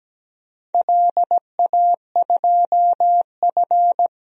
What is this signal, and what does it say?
Morse code, heard as a keyed tone, about 17 wpm, 700 Hz. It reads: LA2F
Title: DXCC-100 Call Signs - 4 Characters - 17wpm